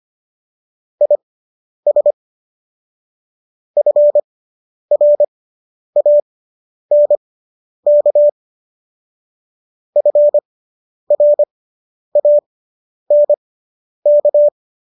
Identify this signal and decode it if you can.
Morse code, heard as a keyed tone, about 25 wpm, spaced as for 10 wpm, 590 Hz. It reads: IS FRANK FRANK